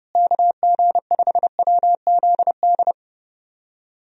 Morse code, heard as a keyed tone, about 30 wpm, 700 Hz. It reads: KG5WZD
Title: USA Call Signs - 6 Characters - Repeated 2x - 30wpm